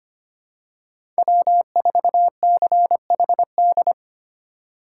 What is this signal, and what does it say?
Morse code, heard as a keyed tone, about 25 wpm, 700 Hz. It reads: W4CHD
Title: USA Call Signs - 5 Characters - 4x Character Spacing - 25wpm